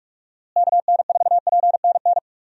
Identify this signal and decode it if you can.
Morse code, heard as a keyed tone, about 41 wpm, 700 Hz. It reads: KN4PNN
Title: USA Call Signs - 6 Characters - Repeated 3x - 45wpm